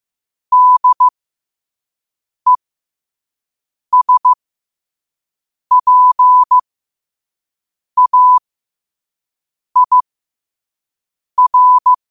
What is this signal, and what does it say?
Morse code, heard as a keyed tone, about 15 wpm, spaced as for 5 wpm, 1000 Hz. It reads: DESPAIR
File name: qotdfiles / 5wpm